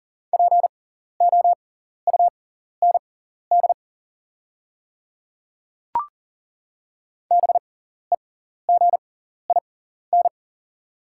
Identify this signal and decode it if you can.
Morse code, heard as a keyed tone, about 40 wpm, spaced as for 14 wpm, 690 Hz. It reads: POUND BEGIN
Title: 6x Character Spacing - Top 500 Words - 40wpm